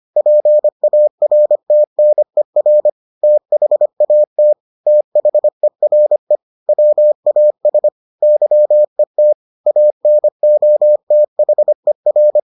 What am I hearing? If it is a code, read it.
Morse code, heard as a keyed tone, about 25 wpm, 600 Hz. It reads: PARTNER THAT THERE WAS YET ANOTHER